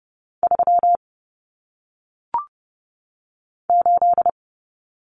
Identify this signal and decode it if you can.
Morse code, heard as a keyed tone, about 30 wpm, 700 Hz. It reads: 3 8
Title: Common Cut Numbers - 1-Digit 30wpm